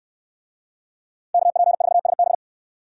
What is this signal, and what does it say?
Morse code, heard as a keyed tone, about 65 wpm, 700 Hz. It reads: KP4SB